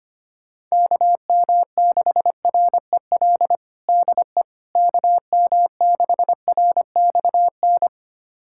Morse code, heard as a keyed tone, about 25 wpm, 700 Hz. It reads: KM6REL DE KM6RXN
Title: USA Callsign DE Callsign - 6 Characters - 25wpm